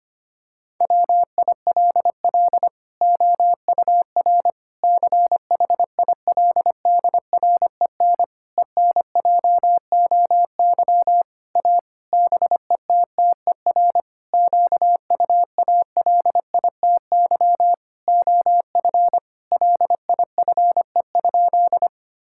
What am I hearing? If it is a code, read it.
Morse code, heard as a keyed tone, about 25 wpm, 700 Hz. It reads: WILL OUR CHILDREN ENJOY A BETTER QUALITY OF LIFE?
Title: Sentences from Top 1000 Words - Review Part 5 - 25wpm